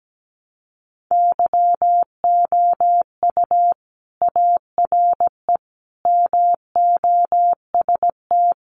Morse code, heard as a keyed tone, about 17 wpm, 700 Hz. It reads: YOU ARE MOST
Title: Sentences from Top 200 Words 17wpm